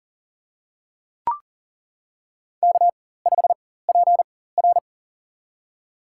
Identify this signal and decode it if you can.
Morse code, heard as a keyed tone, about 40 wpm, spaced as for 19 wpm, 700 Hz. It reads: K5PR
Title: USA Call Signs - 4x Character Spacing - 40wpm